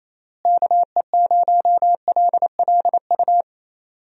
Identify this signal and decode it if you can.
Morse code, heard as a keyed tone, about 28 wpm, 700 Hz. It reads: KE0LLU
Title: USA Call Signs - 6 Characters - Repeated 3x - 28wpm